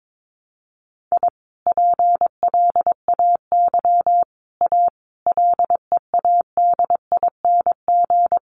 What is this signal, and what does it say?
Morse code, heard as a keyed tone, about 22 wpm, 700 Hz. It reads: I PLAY A LEADING
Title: Sentences from Top 1100 Words 22wpm